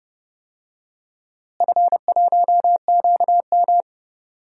Morse code, heard as a keyed tone, about 32 wpm, 700 Hz. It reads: F1QM
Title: DXCC-100 Call Signs - 4 Characters - Repeated 2x - 30wpm